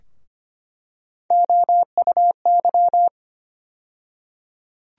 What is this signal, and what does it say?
Morse code, heard as a keyed tone, about 25 wpm, 700 Hz. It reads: OUY
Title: USA Call Sign Postfixes - 3 Letters 25wpm